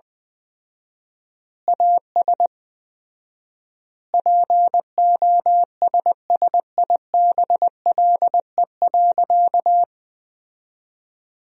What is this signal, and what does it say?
Morse code, heard as a keyed tone, about 20 wpm, 700 Hz. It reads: AS POSSIBLE.
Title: 4x Word-Spacing - Sentences from Top 500 Words - 20wpm